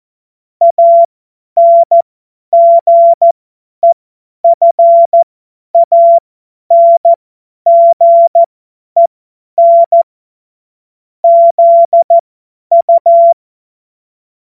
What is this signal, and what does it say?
Morse code, heard as a keyed tone, about 14 wpm, spaced as for 10 wpm, 680 Hz. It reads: ANGEFANGEN ZU
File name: Morsekurs / Witze / Witz 06. Juli 2021, 50 F70